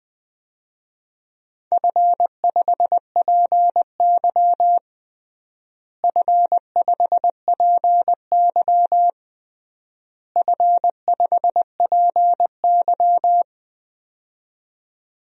Morse code, heard as a keyed tone, about 20 wpm, 700 Hz. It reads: F5PY F5PY F5PY
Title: DXCC-100 Call Signs - 4 Characters - Repeated 3x - 20wpm